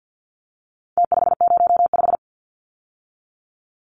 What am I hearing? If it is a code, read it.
Morse code, heard as a keyed tone, about 49 wpm, 700 Hz. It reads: T505
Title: Numbers - Contest Style - 4-Digit 50wpm